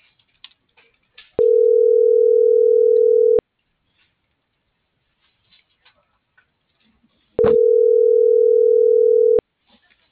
Ringback tone (North America), 440 + 480 Hz.